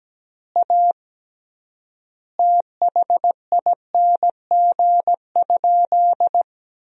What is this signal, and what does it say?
Morse code, heard as a keyed tone, about 17 wpm, 700 Hz. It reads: A THING?
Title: 3x Word-Spacing - Sentences from Top 200 Words - 17wpm